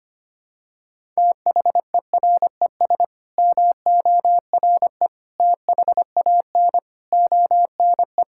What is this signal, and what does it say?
Morse code, heard as a keyed tone, about 25 wpm, 700 Hz. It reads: THERES MORE THAN ONE
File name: Sets of 9 Words - 25wpm